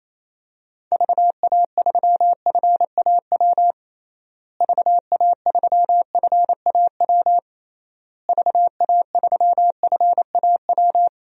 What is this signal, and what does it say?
Morse code, heard as a keyed tone, about 28 wpm, 700 Hz. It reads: VA3FAW VA3FAW VA3FAW
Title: DXCC-100 Call Signs - 6 Characters - Repeated 3x - 28wpm